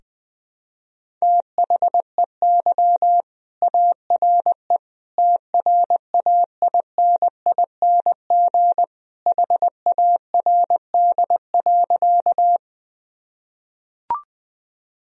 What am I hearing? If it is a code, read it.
Morse code, heard as a keyed tone, about 20 wpm, 700 Hz. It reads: THEY ARE TRAINING HARD.
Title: Sentences from Top 1000 Words - Review Part 4 - 20wpm